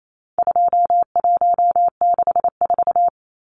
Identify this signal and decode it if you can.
Morse code, heard as a keyed tone, about 28 wpm, 700 Hz. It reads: 2164
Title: Numbers - 4-Digit 28wpm